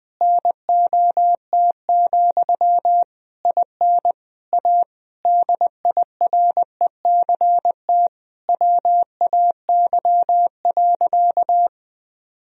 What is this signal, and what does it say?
Morse code, heard as a keyed tone, about 20 wpm, 700 Hz. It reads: NOT, IN A DIRECT WAY.
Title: Sentences from Top 1000 Words - Review Part 2 - 20wpm